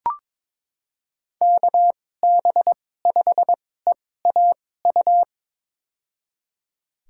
Morse code, heard as a keyed tone, about 22 wpm, spaced as for 16 wpm, 700 Hz. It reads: KB5EAU
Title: USA Call Signs - 6 Characters - 2x Character Spacing - 22wpm